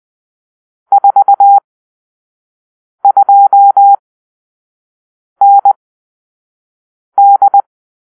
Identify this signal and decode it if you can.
Morse code, heard as a keyed tone, about 20 wpm, spaced as for 5 wpm, 800 Hz. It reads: 42ND
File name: arrl / Grp-6 Drill-3